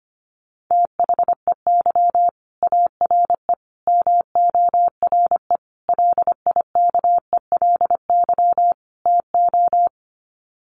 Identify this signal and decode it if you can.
Morse code, heard as a keyed tone, about 25 wpm, 700 Hz. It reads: THEY ARE MORE LIKELY TO
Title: Sets of 5 Words Encore2 25wpm